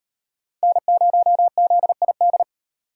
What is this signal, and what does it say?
Morse code, heard as a keyed tone, about 36 wpm, 700 Hz. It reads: N0ZID